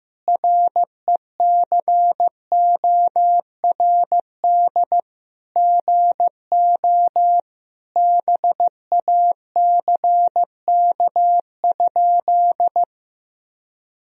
Morse code, heard as a keyed tone, about 15 wpm, 700 Hz. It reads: RECORD GO BACK?
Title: Sentences from Top 1000 Words - Review Part 4 - 15wpm